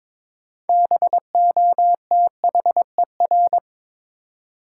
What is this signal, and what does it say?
Morse code, heard as a keyed tone, about 23 wpm, 700 Hz. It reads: BOTHER
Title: Six Letter Words - 22wpm